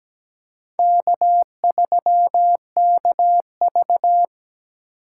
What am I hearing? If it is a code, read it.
Morse code, heard as a keyed tone, about 17 wpm, 700 Hz. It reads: K3KV
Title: USA Call Signs - 2x Character Spacing - 17wpm